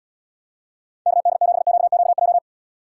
Morse code, heard as a keyed tone, about 78 wpm, 700 Hz. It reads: KD7QQJ